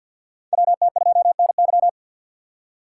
Morse code, heard as a keyed tone, about 52 wpm, 700 Hz. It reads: WT2NY
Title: USA Call Signs - 5 Characters - 4x Character Spacing - 50wpm